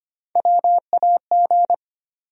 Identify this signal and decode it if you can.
Morse code, heard as a keyed tone, about 25 wpm, 700 Hz. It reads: WAG
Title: USA Call Sign Postfixes - 3 Letters 25wpm